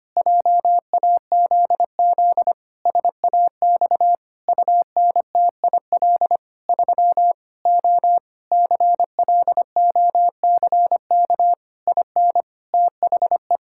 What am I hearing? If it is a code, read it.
Morse code, heard as a keyed tone, about 25 wpm, 700 Hz. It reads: JAZZ SAX UNTIL 3 O CLOCK IN THE
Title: Pangram Challenge v2 - 25wpm